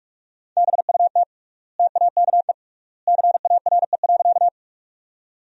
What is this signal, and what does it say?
Morse code, heard as a keyed tone, about 46 wpm, 700 Hz. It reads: BUT TAKE CARE.
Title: 3x Word-Spacing - Sentences from Top 300 Words - 45wpm